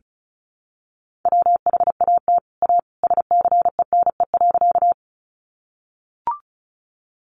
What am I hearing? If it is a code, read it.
Morse code, heard as a keyed tone, about 34 wpm, 700 Hz. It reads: WHAT A SCENE.
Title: Sentences from Top 1100 Words 35wpm